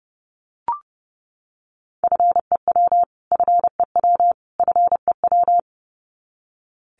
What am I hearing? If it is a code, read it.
Morse code, heard as a keyed tone, about 30 wpm, 700 Hz. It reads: FEW FEW FEW
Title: Top 500 Words Repeated 30wpm